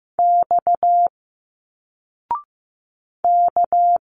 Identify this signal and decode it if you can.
Morse code, heard as a keyed tone, about 15 wpm, 700 Hz. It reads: X K
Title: Single Letters 15wpm